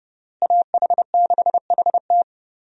Morse code, heard as a keyed tone, about 32 wpm, 700 Hz. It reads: AH6HT
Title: USA Call Signs - 5 Characters - Repeated 3x - 30wpm